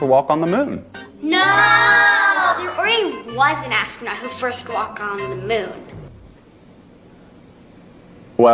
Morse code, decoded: none